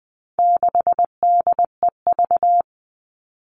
Morse code, heard as a keyed tone, about 20 wpm, 700 Hz. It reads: 6DEV
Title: All Cut Numbers - 4-Digit 20wpm